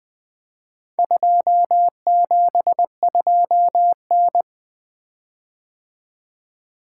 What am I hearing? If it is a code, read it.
Morse code, heard as a keyed tone, about 24 wpm, 700 Hz. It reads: 272N
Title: Speed-Racing - Four Letter-Number Combinations - 20wpm